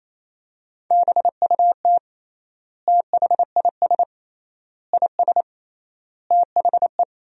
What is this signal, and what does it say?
Morse code, heard as a keyed tone, about 28 wpm, 700 Hz. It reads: BUT THIS IS THE
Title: 3x Word-Spacing - Sentences from Top 200 Words - 28wpm